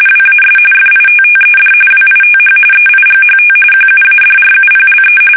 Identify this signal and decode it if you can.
RTTY, about 45.45 baud. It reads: WAS FITTED WITH EIGHT QUADRUPLE
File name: fsk850 / battleshipnj